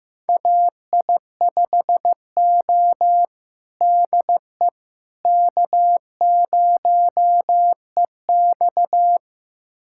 Morse code, heard as a keyed tone, about 15 wpm, 700 Hz. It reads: AI5O DE K0EX
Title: USA Callsign DE Callsign - 4 Characters - 15wpm